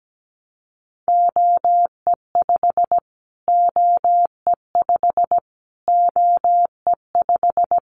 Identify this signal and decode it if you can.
Morse code, heard as a keyed tone, about 17 wpm, 700 Hz. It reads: OE5 OE5 OE5
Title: Callsign Prefixes for Top 25 DXCC - Repeated 3x - 17wpm